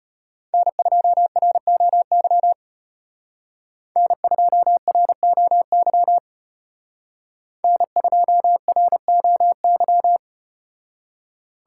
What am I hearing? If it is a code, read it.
Morse code, heard as a keyed tone, about 33 wpm, 700 Hz. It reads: N2ROY N2ROY N2ROY